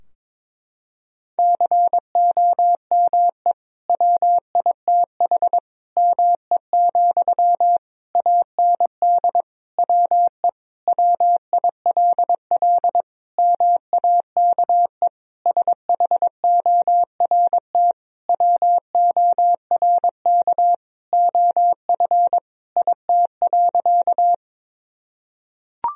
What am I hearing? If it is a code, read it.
Morse code, heard as a keyed tone, about 22 wpm, 700 Hz. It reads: COME WITH ME, AND WE WILL MAKE SHORT WORK OF IT.
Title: Sentences from Top 1000 Words - Review Part 2 - 22wpm